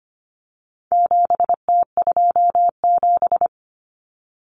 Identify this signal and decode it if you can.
Morse code, heard as a keyed tone, about 25 wpm, 700 Hz. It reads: 7T27
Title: Common Cut Numbers - 4-Digit 25wpm